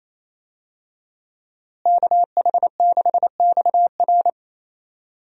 Morse code, heard as a keyed tone, about 28 wpm, 700 Hz. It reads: KH6XR
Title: DXCC-100 Call Signs - 5 Characters - Repeated 3x - 28wpm